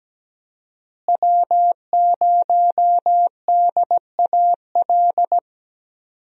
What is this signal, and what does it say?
Morse code, heard as a keyed tone, about 17 wpm, 700 Hz. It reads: W0DAL
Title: USA Call Signs - 5 Characters - Repeated 3x - 17wpm